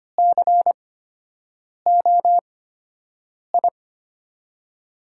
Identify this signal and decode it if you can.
Morse code, heard as a keyed tone, about 25 wpm, spaced as for 7 wpm, 700 Hz. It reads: COI
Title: USA Call Signs - 5 Characters - 8x Character Spacing - 25wpm